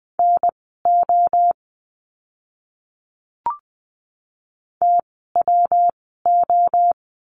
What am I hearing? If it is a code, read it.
Morse code, heard as a keyed tone, about 20 wpm, spaced as for 15 wpm, 700 Hz. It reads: NO TWO
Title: 146 Lesson 24f - Letter y - Words 20wpm